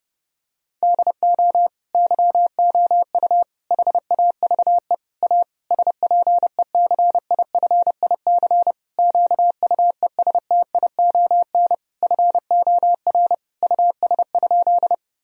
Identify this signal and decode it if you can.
Morse code, heard as a keyed tone, about 30 wpm, 700 Hz. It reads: DO YOU HAVE A SPECIFIC QUESTION FOR US?
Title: Sentences from Top 1000 Words - Review Part 3 - 30wpm